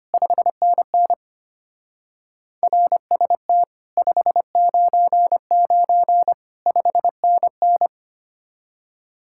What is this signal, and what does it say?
Morse code, heard as a keyed tone, about 29 wpm, 700 Hz. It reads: 5NN RST 599 5NN
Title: Speed-Racing Common QSO Phrases Encore - 25wpm